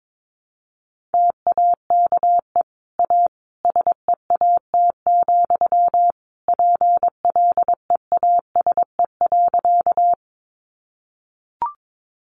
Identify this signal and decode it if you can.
Morse code, heard as a keyed tone, about 22 wpm, 700 Hz. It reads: TAKE A SEAT, PLEASE.
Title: Sentences from Top 1500 Words 22wpm